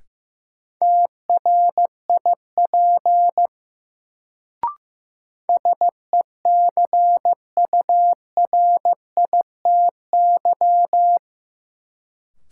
Morse code, heard as a keyed tone, about 15 wpm, 700 Hz. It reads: TRIP SECURITY
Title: Sentences from Top 1000 Words - Review Part 4 - New Word Drill - 15wpm